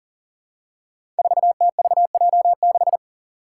Morse code, heard as a keyed tone, about 40 wpm, 700 Hz. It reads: 4TVJ6
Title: Five Challenging Letter-Number Combinations 40wpm